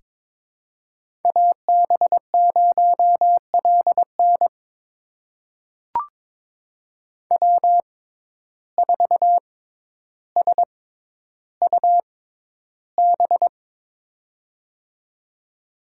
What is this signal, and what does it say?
Morse code, heard as a keyed tone, about 22 wpm, spaced as for 11 wpm, 700 Hz. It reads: AB0LN W4SUB
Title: USA Call Signs - 5 Characters - 6x Character Spacing - 22wpm